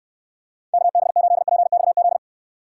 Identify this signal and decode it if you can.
Morse code, heard as a keyed tone, about 69 wpm, 700 Hz. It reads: KB8PXZ